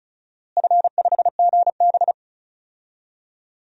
Morse code, heard as a keyed tone, about 36 wpm, 700 Hz. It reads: F5GB
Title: DXCC-100 Call Signs - 4 Characters - Repeated 3x - 35wpm